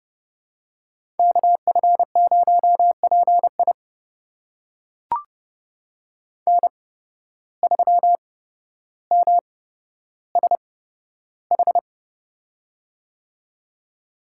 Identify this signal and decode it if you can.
Morse code, heard as a keyed tone, about 30 wpm, spaced as for 13 wpm, 700 Hz. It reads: KF0PI N3MSH
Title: USA Call Signs - 5 Characters - 8x Character Spacing - 30wpm